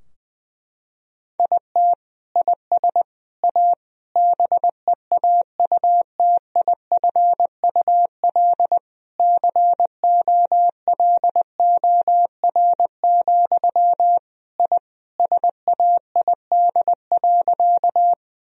Morse code, heard as a keyed tone, about 20 wpm, 700 Hz. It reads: IT IS A BEAUTIFUL COLOR, I SAID.
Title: Sentences from Top 1000 Words - Review Part 2 - 20wpm